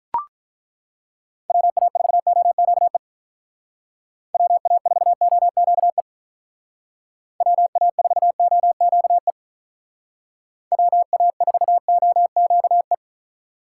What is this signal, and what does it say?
Morse code, heard as a keyed tone, about 42 wpm, 700 Hz. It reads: WA4OQE WA4OQE WA4OQE WA4OQE
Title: Speed Racing - USA Call Signs 35wpm